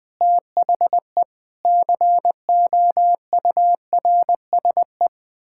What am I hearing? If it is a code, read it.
Morse code, heard as a keyed tone, about 20 wpm, 700 Hz. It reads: THE COURSE
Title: Sentences with 6-Word Phrases - 20wpm